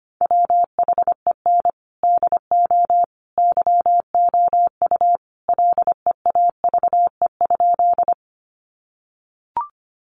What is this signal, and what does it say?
Morse code, heard as a keyed tone, about 25 wpm, 700 Hz. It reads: WHEN DO YOU LEAVE?
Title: Sentences from Top 400 Words 25wpm